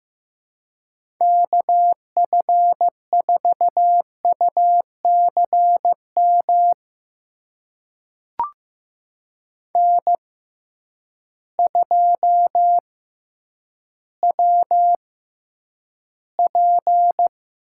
Morse code, heard as a keyed tone, about 15 wpm, spaced as for 9 wpm, 700 Hz. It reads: KF4UCM N2WP
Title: USA Call Signs - 6x Character Spacing - 15wpm